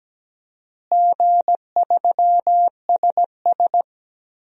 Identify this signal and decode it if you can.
Morse code, heard as a keyed tone, about 17 wpm, 700 Hz. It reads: G3SS
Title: DXCC-100 Call Signs - 4 Characters - Repeated 3x - 17wpm